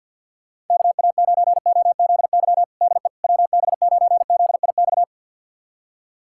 Morse code, heard as a keyed tone, about 46 wpm, 690 Hz. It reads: KA9OZY DE WB9ZIX
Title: USA Callsign DE Callsign - 6 Characters - 50wpm